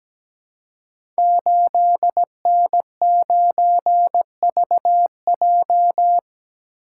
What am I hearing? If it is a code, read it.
Morse code, heard as a keyed tone, about 17 wpm, 700 Hz. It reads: 8N9VJ
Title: Code Groups - 5 Letter:Numbers - 17wpm